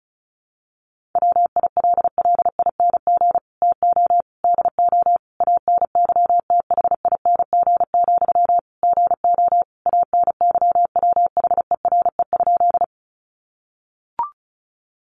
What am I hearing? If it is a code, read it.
Morse code, heard as a keyed tone, about 34 wpm, 700 Hz. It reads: WILLING TO DO ANYTHING, GO ANYWHERE?
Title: Sentences from Top 1400 Words 35wpm